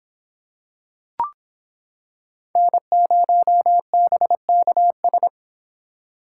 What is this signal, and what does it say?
Morse code, heard as a keyed tone, about 25 wpm, 700 Hz. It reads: N0BKS